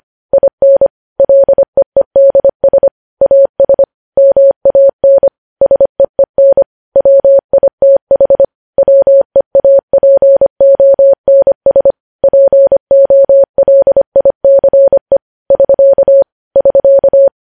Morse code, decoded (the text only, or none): IN LEEDS AS MAN SEEN WITH WEAPONS POLICE <SK> <SK>